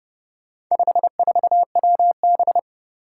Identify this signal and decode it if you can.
Morse code, heard as a keyed tone, about 30 wpm, 700 Hz. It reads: 54WB
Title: Four Challenging Letter-Number Combinations 30wpm